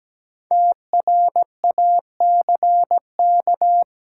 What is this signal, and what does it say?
Morse code, heard as a keyed tone, about 17 wpm, 700 Hz. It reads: TRACK
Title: Sentences from Top 1100 Words - New Word Drill - 17wpm